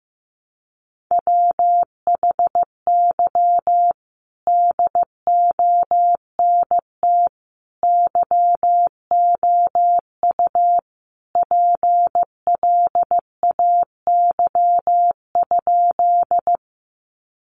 Morse code, heard as a keyed tone, about 15 wpm, 700 Hz. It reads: WHY DONT YOU PLAY?
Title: Sentences from Top 300 Words 15wpm